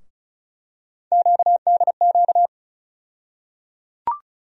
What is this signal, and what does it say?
Morse code, heard as a keyed tone, about 36 wpm, 700 Hz. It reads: QDQ
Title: USA Call Sign Postfixes - 3 Letters 35wpm